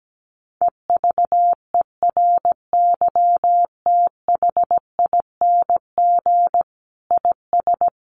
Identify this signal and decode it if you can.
Morse code, heard as a keyed tone, about 17 wpm, 700 Hz. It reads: EVERYTHING IS
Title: Sentences from Top 1400 Words 17wpm